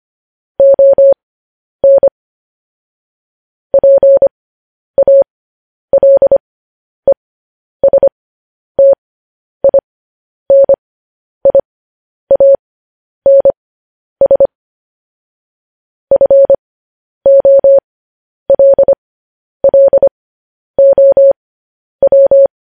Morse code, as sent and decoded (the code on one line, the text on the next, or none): --- -. / .--. .- .-.. ... - .. -. .. .- -. ... / ..-. --- .-.. .-.. --- .--
ON PALESTINIANS FOLLOW